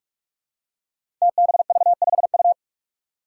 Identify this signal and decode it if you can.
Morse code, heard as a keyed tone, about 43 wpm, 700 Hz. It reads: TBV5U
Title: Five Challenging Letter-Number Combinations 45wpm